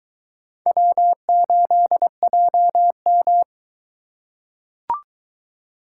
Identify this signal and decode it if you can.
Morse code, heard as a keyed tone, about 22 wpm, 700 Hz. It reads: W8JM